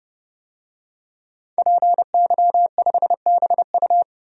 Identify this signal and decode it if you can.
Morse code, heard as a keyed tone, about 32 wpm, 700 Hz. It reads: PY5BU
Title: DXCC-100 Call Signs - 5 Characters - 30wpm